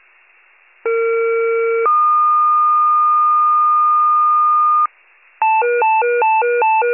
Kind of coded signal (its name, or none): two-tone page